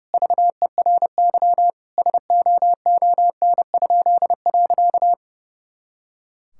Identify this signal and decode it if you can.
Morse code, heard as a keyed tone, about 30 wpm, 700 Hz. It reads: VERY SOON?.
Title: Pangram Challenge v2 - 30wpm